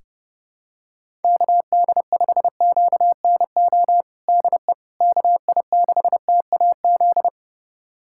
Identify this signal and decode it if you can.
Morse code, heard as a keyed tone, about 32 wpm, 700 Hz. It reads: KD5QNO DE KI6TAZ
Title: USA Callsign DE Callsign 30wpm